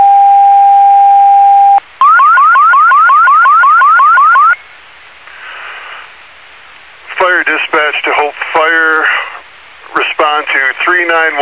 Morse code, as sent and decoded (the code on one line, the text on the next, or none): none